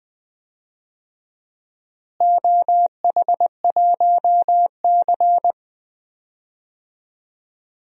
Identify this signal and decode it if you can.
Morse code, heard as a keyed tone, about 20 wpm, 700 Hz. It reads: OH1C
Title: DXCC-100 Call Signs - 4 Characters - Repeated 3x - 20wpm